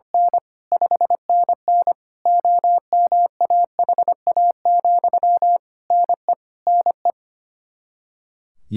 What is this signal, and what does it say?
Morse code, heard as a keyed tone, about 25 wpm, 700 Hz. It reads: N 5NN OMAHA, NE NE
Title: Practice QSO - Exchange 1 - Beginning - 25wpm